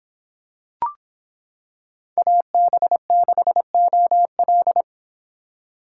Morse code, heard as a keyed tone, about 26 wpm, 700 Hz. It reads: AB6OL